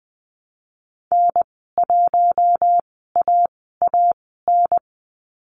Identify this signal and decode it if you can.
Morse code, heard as a keyed tone, about 20 wpm, spaced as for 14 wpm, 700 Hz. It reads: N1AAN
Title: 048 Lesson 9g - Number 4 - Call Signs 20wpm